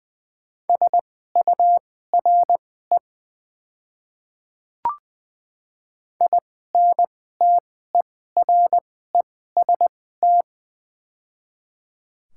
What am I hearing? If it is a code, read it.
Morse code, heard as a keyed tone, about 20 wpm, spaced as for 15 wpm, 700 Hz. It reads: SURE INTEREST
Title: 2x Character Spacing - Top 500 Words - 20wpm